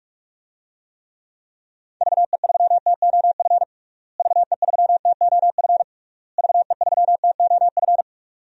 Morse code, heard as a keyed tone, about 46 wpm, 700 Hz. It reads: VE3TOF VE3TOF VE3TOF
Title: DXCC-100 Call Signs - 6 Characters - Repeated 3x - 45wpm